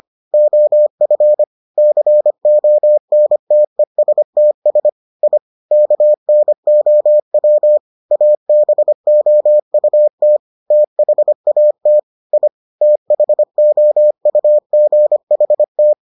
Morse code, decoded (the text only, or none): OF CONTESTS I KNOW ABOUT THAT I THOUGHT